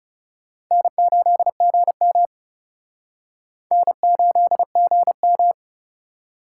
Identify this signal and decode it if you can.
Morse code, heard as a keyed tone, about 36 wpm, 700 Hz. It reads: N8GM N8GM
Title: Speed Racing - USA 1x2 Call Signs 30wpm